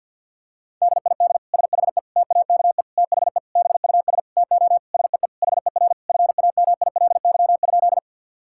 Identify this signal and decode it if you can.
Morse code, heard as a keyed tone, about 50 wpm, 700 Hz. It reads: DID SHE TAKE THE BUS TO SEE HER FAMILY?
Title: Sentences from Top 1000 Words - Review Part 6 - 50wpm